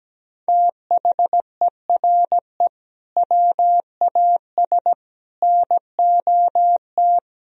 Morse code, heard as a keyed tone, about 17 wpm, 700 Hz. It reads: THERE WAS NOT
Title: Sentences from Top 1000 Words - Review Part 2 - 17wpm